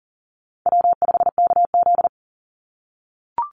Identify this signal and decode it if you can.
Morse code, heard as a keyed tone, about 40 wpm, 690 Hz. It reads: W5KZ
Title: USA Call Signs 40wpm